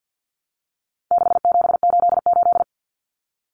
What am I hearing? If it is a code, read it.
Morse code, heard as a keyed tone, about 50 wpm, 700 Hz. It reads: 6788
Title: Numbers - 4-Digit 50wpm